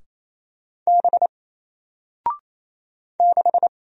Morse code, heard as a keyed tone, about 27 wpm, 700 Hz. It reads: B 6
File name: Single-Letter-Number-28wpm